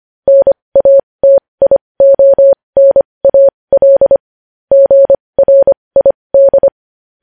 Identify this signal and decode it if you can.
Morse code, heard as a keyed tone, about 25 wpm, spaced as for 20 wpm, 550 Hz. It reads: NATIONAL GRID
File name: podcast / News headlines-1x-25x20